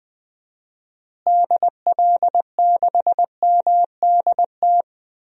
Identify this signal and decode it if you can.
Morse code, heard as a keyed tone, about 20 wpm, 700 Hz. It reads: DL6MDT
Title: DXCC-100 Call Signs - 6 Characters - Repeated 2x - 20wpm